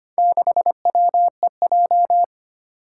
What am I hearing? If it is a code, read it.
Morse code, heard as a keyed tone, about 25 wpm, 700 Hz. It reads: 6WEJ
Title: Four Challenging Letter-Number Combinations 25wpm